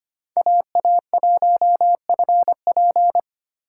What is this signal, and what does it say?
Morse code, heard as a keyed tone, about 25 wpm, 700 Hz. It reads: AA1FP
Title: USA Call Signs - 2x Character Spacing - 25wpm